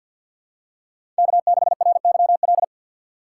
Morse code, heard as a keyed tone, about 53 wpm, 700 Hz. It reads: K6RYL
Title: USA Call Signs - 5 Characters - 6x Character Spacing - 50wpm